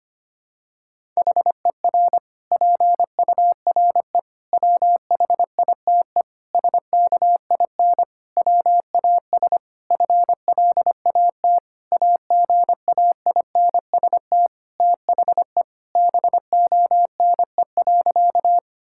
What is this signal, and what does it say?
Morse code, heard as a keyed tone, about 25 wpm, 700 Hz. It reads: HER PURE WHITE SKIN WAS FLAT AGAINST THE BONE.
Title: Sentences from Top 1600 Words 25wpm